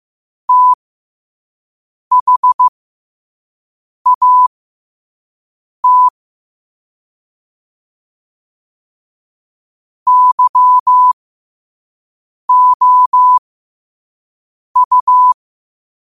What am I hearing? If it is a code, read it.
Morse code, heard as a keyed tone, about 15 wpm, spaced as for 5 wpm, 1000 Hz. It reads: THAT YOU
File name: qotdfiles / 5wpm